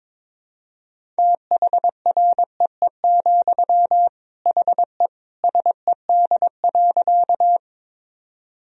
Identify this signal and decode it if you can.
Morse code, heard as a keyed tone, about 22 wpm, 700 Hz. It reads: THREE, HE SED.
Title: Sentences from Top 200 Words and Abbreviations 22wpm